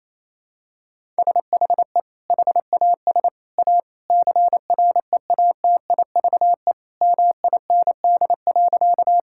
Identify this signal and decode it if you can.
Morse code, heard as a keyed tone, about 28 wpm, 700 Hz. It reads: SHE HAS A CREATIVE MIND.
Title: Sentences from Top 1000 Words - Review Part 5 - 28wpm